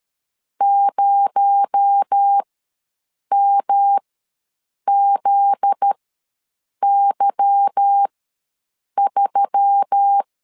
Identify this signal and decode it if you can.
Morse code, heard as a keyed tone, about 13 wpm, spaced as for 7 wpm, 800 Hz. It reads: 0MZY3
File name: LEZIONI / 10 SIGNS / 10A SLASH ws 12wpm 3443